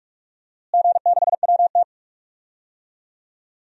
Morse code, heard as a keyed tone, about 46 wpm, 700 Hz. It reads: G6WT
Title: DXCC-100 Call Signs - 4 Characters - Repeated 2x - 45wpm